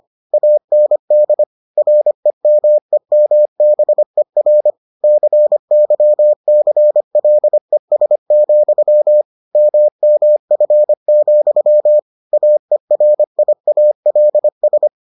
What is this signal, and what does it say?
Morse code, heard as a keyed tone, about 25 wpm, 600 Hz. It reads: AND REMEMBER CYCLES, MMF, AERIALS